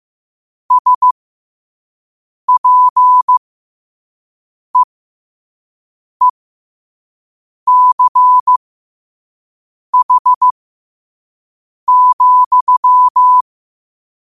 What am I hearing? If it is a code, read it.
Morse code, heard as a keyed tone, about 15 wpm, spaced as for 5 wpm, 1000 Hz. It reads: SPEECH,
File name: qotdfiles / 5wpm